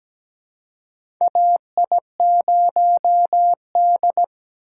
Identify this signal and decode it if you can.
Morse code, heard as a keyed tone, about 17 wpm, 700 Hz. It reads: AI0D
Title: USA Call Signs - 4 Characters - 3x Character Spacing - 17wpm